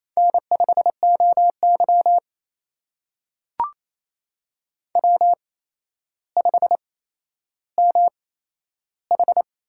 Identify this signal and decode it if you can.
Morse code, heard as a keyed tone, about 28 wpm, spaced as for 12 wpm, 700 Hz. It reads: N5OY W5MH
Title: USA Call Signs - 4 Characters - 8x Character Spacing - 28wpm